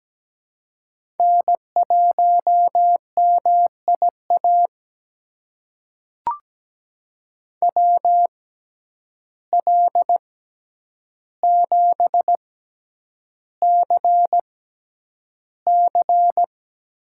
Morse code, heard as a keyed tone, about 17 wpm, spaced as for 9 wpm, 700 Hz. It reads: N1MIA WL7CC
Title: USA Call Signs - 5 Characters - 6x Character Spacing - 17wpm